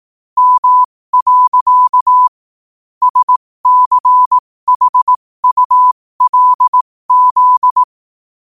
Morse code, heard as a keyed tone, about 18 wpm, spaced as for 15 wpm, 1000 Hz. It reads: M. SCHULZ
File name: qotdfiles / 15wpm